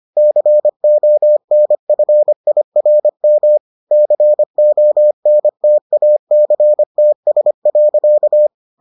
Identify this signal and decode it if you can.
Morse code, heard as a keyed tone, about 25 wpm, 600 Hz. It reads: CONFIRM CONTACTS.